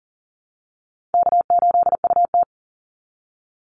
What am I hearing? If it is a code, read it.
Morse code, heard as a keyed tone, about 40 wpm, 700 Hz. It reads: K8UT
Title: Easy Call Signs Repeated 3x - 40wpm